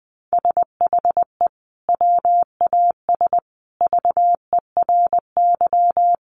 Morse code, heard as a keyed tone, about 20 wpm, 700 Hz. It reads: SHE WAS VERY
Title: Sentences from Top 900 Words 20wpm